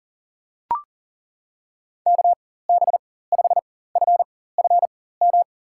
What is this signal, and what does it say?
Morse code, heard as a keyed tone, about 40 wpm, spaced as for 19 wpm, 700 Hz. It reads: KB5FFM
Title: USA Call Signs - 6 Characters - 4x Character Spacing - 40wpm